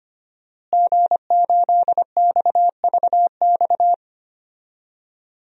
Morse code, heard as a keyed tone, about 25 wpm, 700 Hz. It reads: G8XVX
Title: DXCC-100 Call Signs - 5 Characters - Repeated 2x - 25wpm